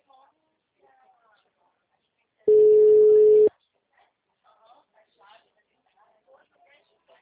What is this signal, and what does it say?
Ringback tone (Europe, 425 Hz), with speech.
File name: out- -2005- - - .78167